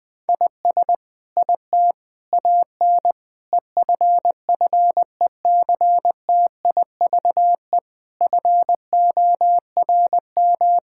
Morse code, heard as a keyed tone, about 20 wpm, 700 Hz. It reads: IS IT AN EFFECTIVE FORM
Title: Sentences from Top 1000 Words - Review Part 6 - 20wpm